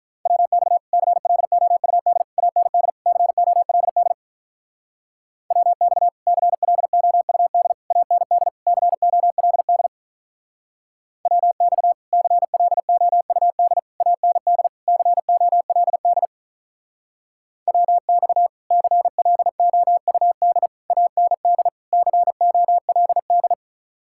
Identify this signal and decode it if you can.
Morse code, heard as a keyed tone, about 46 wpm, 700 Hz. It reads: WX CLOUD AND COLD WX CLOUD AND COLD WX CLOUD AND COLD WX CLOUD AND COLD